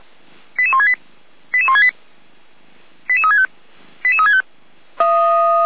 Five-tone selective call (ZVEI), heard as quick runs of five tones, tones of about 70 ms. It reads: 88168 88168 88365 88365